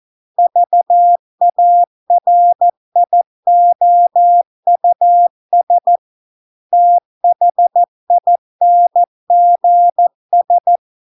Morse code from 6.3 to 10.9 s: THINGS